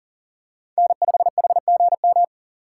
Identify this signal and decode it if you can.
Morse code, heard as a keyed tone, about 37 wpm, 700 Hz. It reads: N5HGM
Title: USA Call Signs - 5 Characters - 6x Character Spacing - 40wpm